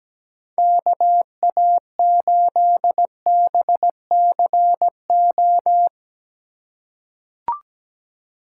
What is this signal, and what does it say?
Morse code, heard as a keyed tone, about 17 wpm, 700 Hz. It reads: KA8BCO
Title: USA Call Signs - 6 Characters - 2x Character Spacing - 17wpm